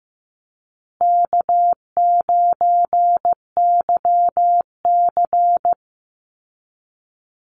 Morse code, heard as a keyed tone, about 15 wpm, 700 Hz. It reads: K9YC
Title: Easy Call Signs Repeated 2x - 15wpm